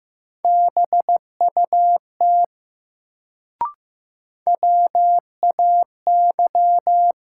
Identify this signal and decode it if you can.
Morse code, heard as a keyed tone, about 15 wpm, 700 Hz. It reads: BUT WAY
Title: Sets of 2 Words - New Word Drill - 15wpm